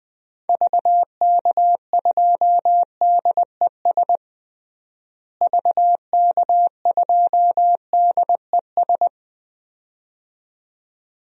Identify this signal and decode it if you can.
Morse code, heard as a keyed tone, about 20 wpm, 700 Hz. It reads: VK2DES VK2DES
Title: DXCC-100 Call Signs - 6 Characters - Repeated 2x - 20wpm